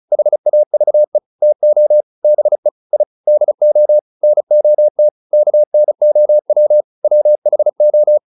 Morse code, heard as a keyed tone, about 34 wpm, 600 Hz. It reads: HAVE TO BE I DO NOT KNOW WHO